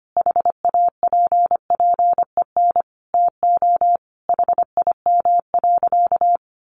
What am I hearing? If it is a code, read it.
Morse code, heard as a keyed tone, about 25 wpm, 700 Hz. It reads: HAPPEN TO HIM.
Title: Sentences from Top 400 Words 25wpm